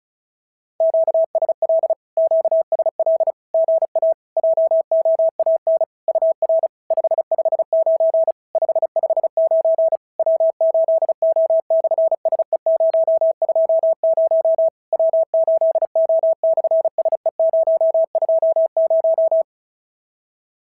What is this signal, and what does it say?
Morse code, heard as a keyed tone, about 35 wpm, 650 Hz. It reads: QSL QSL GA JOAN UR 559 559 W8O/SE020 W8O/SE020
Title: SOTA QSO Practice 35wpm